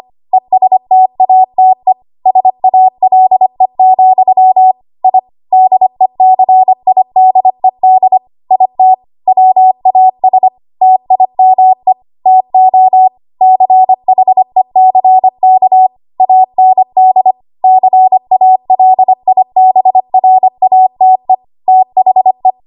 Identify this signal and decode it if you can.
Morse code, heard as a keyed tone, about 25 wpm, 750 Hz. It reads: ESTATE SALE, I DECIDED IT WAS TIME TO CHECK AND CALIBRATE THE
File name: Archive / 25 WPM / 25WPM